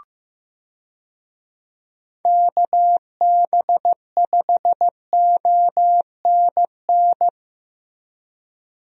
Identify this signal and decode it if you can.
Morse code, heard as a keyed tone, about 15 wpm, 700 Hz. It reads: KB5ONN
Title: USA Call Signs - 6 Characters - Repeated 3x - 15wpm